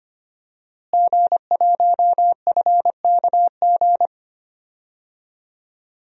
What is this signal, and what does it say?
Morse code, heard as a keyed tone, about 25 wpm, 700 Hz. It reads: G1FKG
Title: DXCC-100 Call Signs - 5 Characters - Repeated 2x - 25wpm